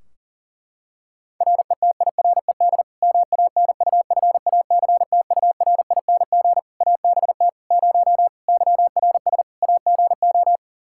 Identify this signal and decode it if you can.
Morse code, heard as a keyed tone, about 40 wpm, 700 Hz. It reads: RETIRED MANUFACTURING ABT 0 YRS AGO
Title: Practice QSO - Exchange 3 - More Info - 40wpm